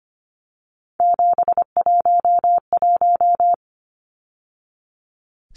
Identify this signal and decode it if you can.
Morse code, heard as a keyed tone, about 24 wpm, 700 Hz. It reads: 711
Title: Numbers - 3-Digit 25wpm